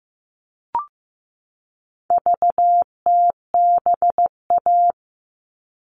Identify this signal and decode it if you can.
Morse code, heard as a keyed tone, about 15 wpm, 700 Hz. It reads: VTBA
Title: All Cut Numbers - 4-Digit 15wpm